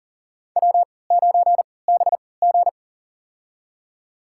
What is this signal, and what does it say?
Morse code, heard as a keyed tone, about 38 wpm, spaced as for 22 wpm, 700 Hz. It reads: W9BG
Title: USA Call Signs - 4 Characters - 3x Character Spacing - 40wpm